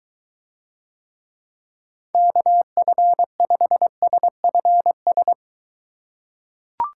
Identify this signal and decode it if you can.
Morse code, heard as a keyed tone, about 23 wpm, 700 Hz. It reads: KF5SFS